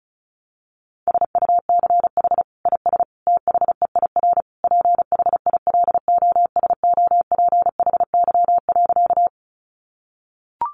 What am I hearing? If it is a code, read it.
Morse code, heard as a keyed tone, about 34 wpm, 700 Hz. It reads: SUCH IS THEIR PHILOSOPHY.
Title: Sentences from Top 900 Words 35wpm